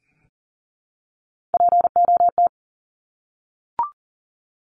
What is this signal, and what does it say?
Morse code, heard as a keyed tone, about 40 wpm, 700 Hz. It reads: POT
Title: Top 751-1000 Words 40wpm